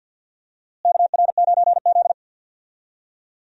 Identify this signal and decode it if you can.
Morse code, heard as a keyed tone, about 51 wpm, 700 Hz. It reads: KR9Z
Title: USA Call Signs - 4 Characters - Repeated 3x - 50wpm